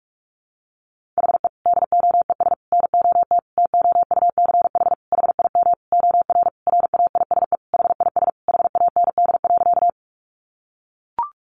Morse code, heard as a keyed tone, about 44 wpm, 700 Hz. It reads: HE DOES NOT TOUCH HIM OR RAISE HIS HAND.
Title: Sentences from Top 1400 Words 45wpm